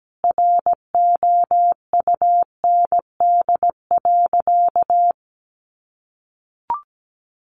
Morse code, heard as a keyed tone, about 17 wpm, 700 Hz. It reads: ROUND.
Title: Sentences from Top 200 Words 17wpm